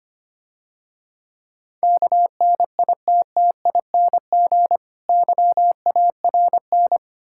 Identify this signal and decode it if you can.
Morse code, heard as a keyed tone, about 27 wpm, 700 Hz. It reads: KNITTING YARN
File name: practice / 25wpm / Difficult Words - Speed-Racing with Hints - 25wpm